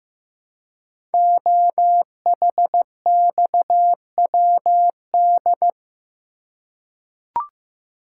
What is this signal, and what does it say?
Morse code, heard as a keyed tone, about 15 wpm, 700 Hz. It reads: OHXWD
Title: Code Groups - 5 Letters - 15wpm